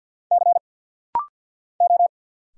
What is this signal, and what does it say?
Morse code, heard as a keyed tone, about 48 wpm, 700 Hz. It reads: C O
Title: Single Letter-Number - Rapid-Fire 50wpm